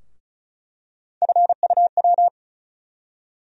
Morse code, heard as a keyed tone, about 36 wpm, 700 Hz. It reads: FUW
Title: USA Call Sign Postfixes - 3 Letters 35wpm